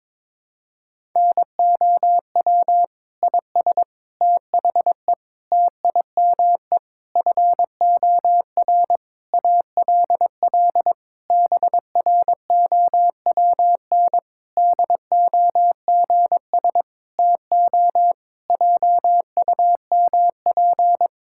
Morse code, heard as a keyed tone, about 22 wpm, 700 Hz. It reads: NOW IS THE TIME FOR ALL BROWN DOGS TO JUMP
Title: Pangram Challenge v2 - 22wpm